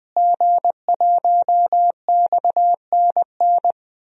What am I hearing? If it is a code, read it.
Morse code, heard as a keyed tone, about 20 wpm, 700 Hz. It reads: G1XNN
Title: DXCC-100 Call Signs - 5 Characters - Repeated 2x - 20wpm